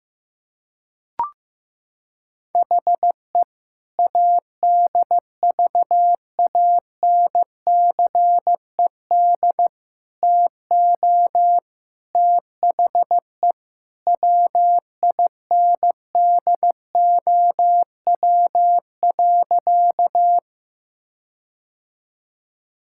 Morse code, heard as a keyed tone, about 15 wpm, 700 Hz. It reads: HE ADVANCED TO THE WINDOW.
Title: Sentences from Top 1600 Words 15wpm